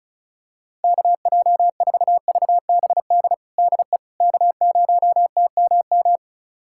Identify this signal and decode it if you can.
Morse code, heard as a keyed tone, about 34 wpm, 700 Hz. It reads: KJ4VBD DE K0TMM
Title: USA Callsign DE Callsign 35wpm